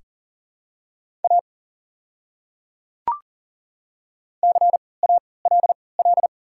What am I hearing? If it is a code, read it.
Morse code, heard as a keyed tone, about 38 wpm, spaced as for 22 wpm, 700 Hz. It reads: A CALL
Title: 3x Character Spacing - Top 200 Words - 40wpm